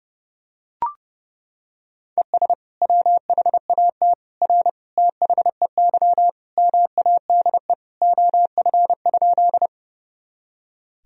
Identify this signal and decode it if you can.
Morse code, heard as a keyed tone, about 30 wpm, 700 Hz. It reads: ES WHAT R THEY MADE OF?
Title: Sentences from Top 200 Words and Abbreviations 30wpm